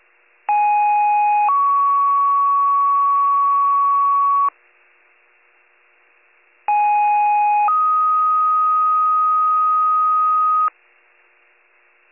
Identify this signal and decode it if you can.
Two-tone page, heard as a one-second tone, then a three-second tone, about 830 then 1120 Hz, about 830 then 1250 Hz. It is heard from 0.5 to 10.7 s.